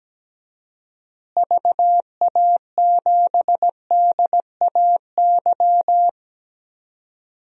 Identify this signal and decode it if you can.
Morse code, heard as a keyed tone, about 17 wpm, 700 Hz. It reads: VA7DAY
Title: DXCC-100 Call Signs - 6 Characters - Repeated 3x - 17wpm